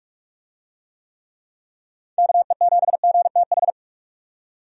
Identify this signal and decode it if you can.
Morse code, heard as a keyed tone, about 46 wpm, 700 Hz. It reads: KE7GTH
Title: USA Call Signs - 6 Characters - 45wpm